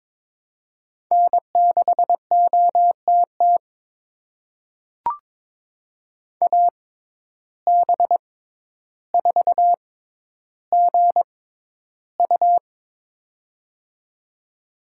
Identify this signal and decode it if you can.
Morse code, heard as a keyed tone, about 22 wpm, spaced as for 11 wpm, 700 Hz. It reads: N6OTT AB4GU
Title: USA Call Signs - 5 Characters - 6x Character Spacing - 22wpm